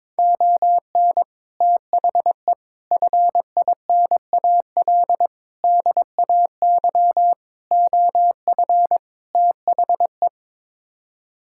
Morse code, heard as a keyed tone, about 21 wpm, 700 Hz. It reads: ON THE FINAL DAY OF THE
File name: practice / 22wpm / Sets of 6 Words - 22wpm